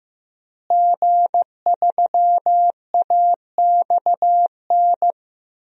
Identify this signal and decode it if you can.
Morse code, heard as a keyed tone, about 15 wpm, 700 Hz. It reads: G3AXN
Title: DXCC-100 Call Signs - 5 Characters - Repeated 3x - 15wpm